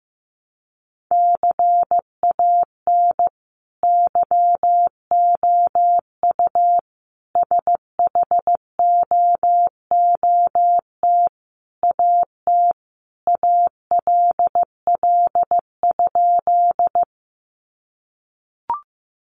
Morse code, heard as a keyed tone, about 15 wpm, 700 Hz. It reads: CAN YOU SHOOT AT ALL?
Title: Sentences from Top 1200 Words 15wpm